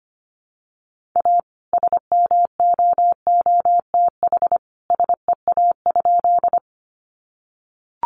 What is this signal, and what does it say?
Morse code, heard as a keyed tone, about 25 wpm, 700 Hz. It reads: A SMOOTH SEA?
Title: Sentences from Top 1400 Words 25wpm